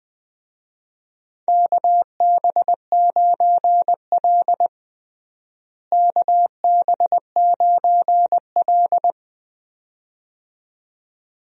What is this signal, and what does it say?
Morse code, heard as a keyed tone, about 20 wpm, 700 Hz. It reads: KB9L KB9L
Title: USA Call Signs - 4 Characters - Repeated 2x - 20wpm